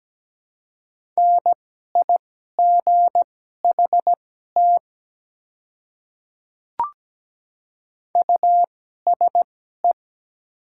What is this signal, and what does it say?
Morse code, heard as a keyed tone, about 17 wpm, spaced as for 12 wpm, 700 Hz. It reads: NIGHT USE
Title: 2x Character Spacing - Top 300 Words - 17wpm